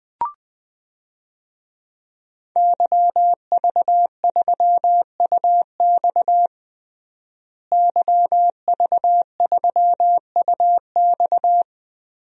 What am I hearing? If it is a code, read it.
Morse code, heard as a keyed tone, about 20 wpm, 700 Hz. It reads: YV3UX YV3UX
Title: DXCC-100 Call Signs - 5 Characters - Repeated 3x - 20wpm